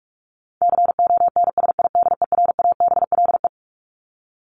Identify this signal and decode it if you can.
Morse code, heard as a keyed tone, about 44 wpm, 700 Hz. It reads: CONSIDERABLE
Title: LY Suffix - Top 100 Words with Base Words 45wpm